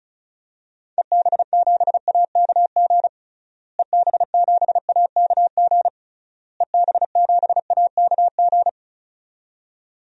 Morse code, heard as a keyed tone, about 36 wpm, 700 Hz. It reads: EB7AKG EB7AKG EB7AKG
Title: DXCC-100 Call Signs - 6 Characters - Repeated 3x - 35wpm